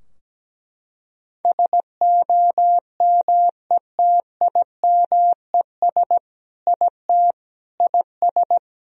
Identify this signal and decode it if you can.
Morse code, heard as a keyed tone, about 17 wpm, 700 Hz. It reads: SOMETIMES IT IS
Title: Sentences from Top 1000 Words - Review Part 3 - 17wpm